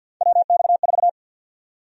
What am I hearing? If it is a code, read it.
Morse code, heard as a keyed tone, about 53 wpm, 700 Hz. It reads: WX4
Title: USA Call Sign Prefixes - 50wpm